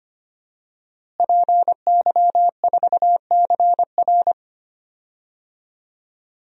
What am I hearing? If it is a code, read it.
Morse code, heard as a keyed tone, about 25 wpm, 700 Hz. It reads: PY4CR
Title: DXCC-100 Call Signs - 5 Characters - Repeated 2x - 25wpm